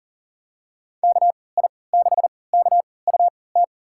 Morse code, heard as a keyed tone, about 40 wpm, spaced as for 23 wpm, 700 Hz. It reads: KI6KUT
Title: USA Call Signs - 6 Characters - 3x Character Spacing - 40wpm